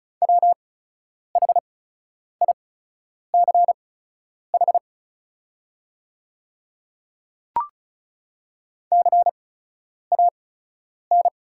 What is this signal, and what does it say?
Morse code, heard as a keyed tone, about 36 wpm, spaced as for 10 wpm, 700 Hz. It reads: WHICH CAN
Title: 8x Character Spacing - Top 100 Words - 35wpm